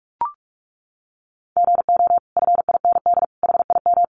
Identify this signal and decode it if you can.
Morse code, heard as a keyed tone, about 44 wpm, 700 Hz. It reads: GO FIND HIM
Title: Sentences from Top 1100 Words 45wpm